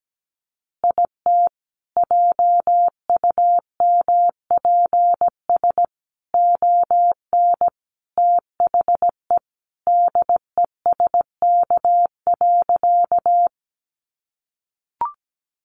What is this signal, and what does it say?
Morse code, heard as a keyed tone, about 17 wpm, 700 Hz. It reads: IT JUMPS ON THE DESK.
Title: Additional Easy Phrases 17wpm